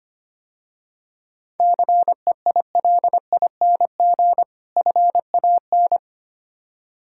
Morse code, heard as a keyed tone, about 28 wpm, 700 Hz. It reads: CEILING FAN
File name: Easy Words - Speed-Racing with Hints - 25wpm